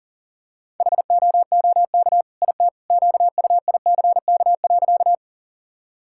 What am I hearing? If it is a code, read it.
Morse code, heard as a keyed tone, about 40 wpm, 700 Hz. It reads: HOOK IT QUICK.
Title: Sentences from Top 1600 Words 40wpm